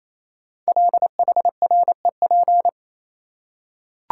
Morse code, heard as a keyed tone, about 28 wpm, 700 Hz. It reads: LHREP
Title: Code Groups - 5 Letters - 28wpm